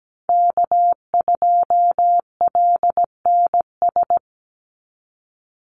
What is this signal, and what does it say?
Morse code, heard as a keyed tone, about 17 wpm, 700 Hz. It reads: K2LNS
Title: Call Signs Repeated 3x - 17wpm